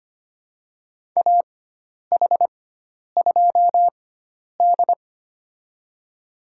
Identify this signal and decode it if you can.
Morse code, heard as a keyed tone, about 25 wpm, spaced as for 10 wpm, 700 Hz. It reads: AH2D
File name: Lesson 16g - Letter c - Call Signs 25wpm